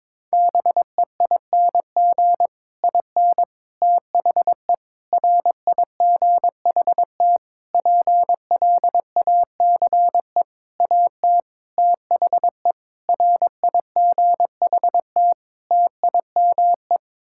Morse code, heard as a keyed tone, about 22 wpm, 700 Hz. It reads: BEING IN THE RIGHT PLACE AT THE RIGHT TIME
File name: Sets of 9 Words - 22wpm